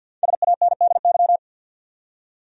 Morse code, heard as a keyed tone, about 52 wpm, 700 Hz. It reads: SANDY
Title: USA Common Names - Max 5 Letters - 50wpm